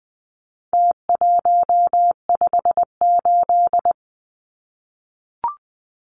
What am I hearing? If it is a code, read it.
Morse code, heard as a keyed tone, about 20 wpm, 700 Hz. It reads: T158
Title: Numbers - Contest Style - 4-Digit 20wpm